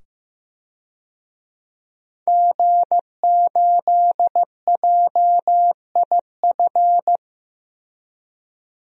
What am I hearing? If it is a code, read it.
Morse code, heard as a keyed tone, about 16 wpm, 700 Hz. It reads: G8JIF
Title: DXCC-100 Call Signs - 5 Characters - 15wpm